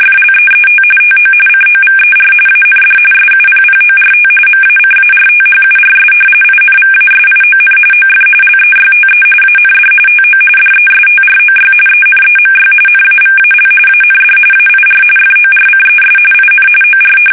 RTTY, about 45.45 baud. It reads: FLEET TO COVER AND SUPPORT MACARTHURS INVASION OF LEYTE AND TO DESTROY ENEMY NAVAL AND AIR FORCES IN OR
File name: fsk850 / battleshipnj